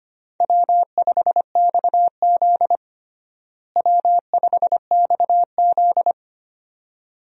Morse code, heard as a keyed tone, about 25 wpm, 700 Hz. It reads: W5XZ W5XZ
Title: USA Call Signs - 4 Characters - Repeated 2x - 25wpm